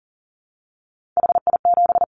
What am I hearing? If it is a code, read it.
Morse code, heard as a keyed tone, about 40 wpm, 700 Hz. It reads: HI7
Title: Callsign Prefixes for Top 100 DXCC - 40wpm